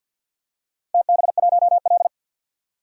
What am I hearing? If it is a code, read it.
Morse code, heard as a keyed tone, about 51 wpm, 700 Hz. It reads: TB1L
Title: DXCC-100 Call Signs - 4 Characters - 50wpm